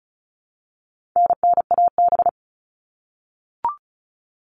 Morse code, heard as a keyed tone, about 34 wpm, 700 Hz. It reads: NNAB
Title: All Cut Numbers - 4-Digit 35wpm